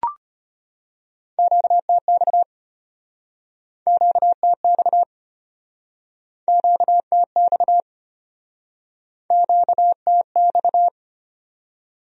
Morse code, heard as a keyed tone, about 36 wpm, 700 Hz. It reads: QTX QTX QTX QTX